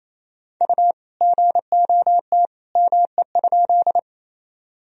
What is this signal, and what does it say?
Morse code, heard as a keyed tone, about 28 wpm, 700 Hz. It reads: U GOT ME?
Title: Sentences from Top 300 Words and Abbreviations 28wpm